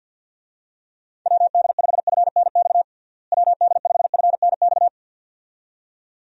Morse code, heard as a keyed tone, about 50 wpm, 700 Hz. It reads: WD5FNX WD5FNX
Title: USA Call Signs - 6 Characters - Repeated 2x - 50wpm